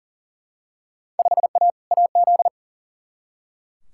Morse code, heard as a keyed tone, about 40 wpm, 700 Hz. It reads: 5A AZ
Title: Field Day Exchanges - 40wpm